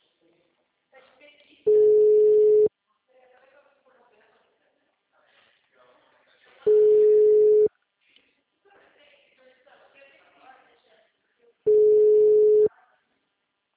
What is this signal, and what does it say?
Ringback tone (Europe, 425 Hz), with speech.
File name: out- -2015- - - .21722